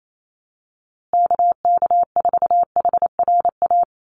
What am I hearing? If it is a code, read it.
Morse code, heard as a keyed tone, about 28 wpm, 700 Hz. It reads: KK4HRA
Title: USA Call Signs Repeated 2x 28wpm